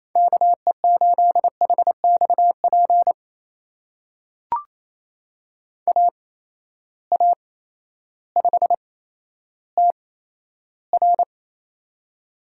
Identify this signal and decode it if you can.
Morse code, heard as a keyed tone, about 27 wpm, spaced as for 13 wpm, 700 Hz. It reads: KE8HXP AA5TR
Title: USA Call Signs - 8x Character Spacing - 28wpm